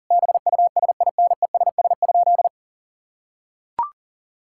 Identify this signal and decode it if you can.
Morse code, heard as a keyed tone, about 40 wpm, 700 Hz. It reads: BUSINESS?
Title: Sentences from Top 1000 Words - Review Part 5 - 40wpm